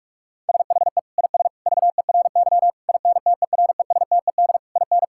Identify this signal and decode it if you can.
Morse code, heard as a keyed tone, about 42 wpm, 700 Hz. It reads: SHE IS VERY INTERESTED IN
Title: Sentences from Top 1000 Words - Review Part 6 - 45wpm